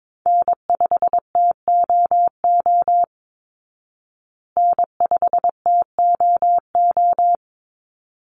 Morse code, heard as a keyed tone, about 22 wpm, 700 Hz. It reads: N5TOO N5TOO
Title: Call Signs Repeated 3x - 22wpm